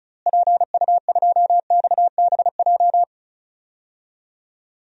Morse code, heard as a keyed tone, about 36 wpm, 700 Hz. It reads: PU2XBJ
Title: DXCC-100 Call Signs - 6 Characters - Repeated 2x - 35wpm